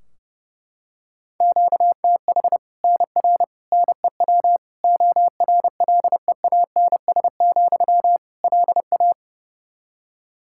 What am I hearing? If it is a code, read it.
Morse code, heard as a keyed tone, about 29 wpm, 700 Hz. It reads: QTH NR NEW ORLEANS, LA
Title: Practice QSO - Exchange 1 - Beginning - 30wpm